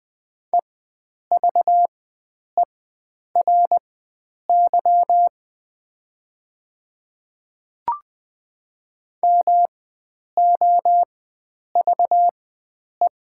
Morse code, heard as a keyed tone, about 20 wpm, spaced as for 9 wpm, 700 Hz. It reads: EVERY MOVE
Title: 4x Character Spacing - Top 300 Words - 20wpm